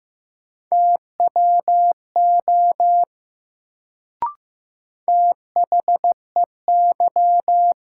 Morse code, heard as a keyed tone, about 15 wpm, 700 Hz. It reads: TWO THEY
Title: Sets of 2 Words - New Word Drill - 15wpm